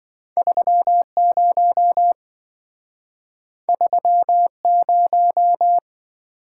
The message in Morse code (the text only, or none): 30 30